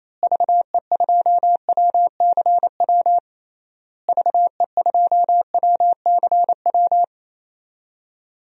Morse code, heard as a keyed tone, about 28 wpm, 700 Hz. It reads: VE2WCW VE2WCW
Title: DXCC-100 Call Signs - 6 Characters - Repeated 3x - 28wpm